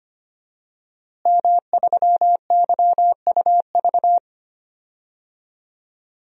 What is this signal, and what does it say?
Morse code, heard as a keyed tone, about 25 wpm, 700 Hz. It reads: M3YUV
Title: DXCC-100 Call Signs - 5 Characters - Repeated 2x - 25wpm